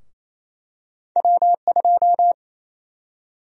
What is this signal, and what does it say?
Morse code, heard as a keyed tone, about 28 wpm, 700 Hz. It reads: W2
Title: USA Call Sign Prefixes - 28wpm